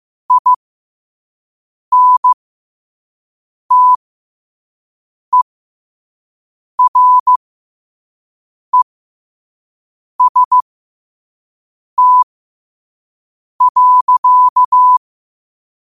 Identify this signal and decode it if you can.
Morse code, heard as a keyed tone, about 15 wpm, spaced as for 5 wpm, 1000 Hz. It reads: INTEREST.
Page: Quotes for Wed, 13 Aug 2025 in Morse Code at 5 words per minute.